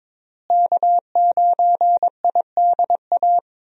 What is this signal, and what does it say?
Morse code, heard as a keyed tone, about 21 wpm, 700 Hz. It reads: K9IDA
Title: USA Call Signs - 5 Characters - Repeated 3x - 22wpm